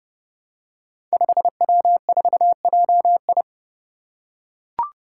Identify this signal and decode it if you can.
Morse code, heard as a keyed tone, about 30 wpm, 700 Hz. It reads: 5W4JI
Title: Five Challenging Letter-Number Combinations 30wpm